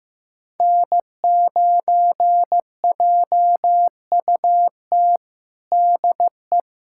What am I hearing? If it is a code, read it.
Morse code, heard as a keyed tone, about 15 wpm, 700 Hz. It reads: N9JUT DE
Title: USA Callsign DE Callsign - 5 Characters - 15wpm